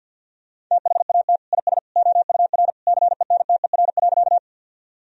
Morse code, heard as a keyed tone, about 52 wpm, 700 Hz. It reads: THAT IS OUR CENTER.
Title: Sentences from Top 1000 Words - Review Part 2 - 50wpm